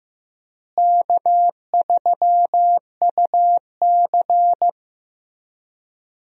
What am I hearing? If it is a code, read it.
Morse code, heard as a keyed tone, about 16 wpm, 700 Hz. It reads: K3UC
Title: USA Call Signs - 4 Characters - Repeated 2x - 15wpm